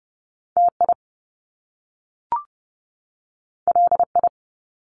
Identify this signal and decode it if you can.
Morse code, heard as a keyed tone, about 30 wpm, 700 Hz. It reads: TI LI
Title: Top Two Letter Combinations in Words 30wpm